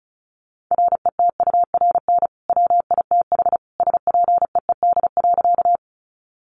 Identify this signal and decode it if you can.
Morse code, heard as a keyed tone, about 34 wpm, 700 Hz. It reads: RETURN WITH SPEED.
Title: Sentences from Top 800 Words 35wpm